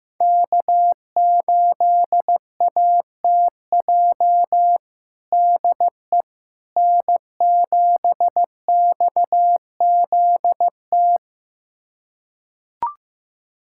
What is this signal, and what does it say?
Morse code, heard as a keyed tone, about 15 wpm, 700 Hz. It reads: K8ATJ DE N7XZT
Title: USA Callsign DE Callsign - 5 Characters - 15wpm